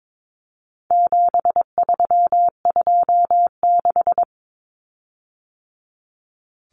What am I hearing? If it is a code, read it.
Morse code, heard as a keyed tone, about 22 wpm, 700 Hz. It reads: 7326
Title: Common Cut Numbers - 4-Digit 22wpm